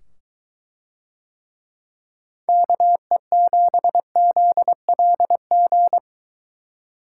Morse code, heard as a keyed tone, about 21 wpm, 700 Hz. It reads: KE7ZLG